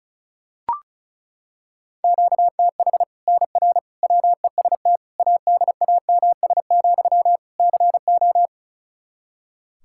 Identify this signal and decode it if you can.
Morse code, heard as a keyed tone, about 36 wpm, 700 Hz. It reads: QTH NR WEST ADAMS, CO
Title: Practice QSO - Exchange 1 - Beginning - 35wpm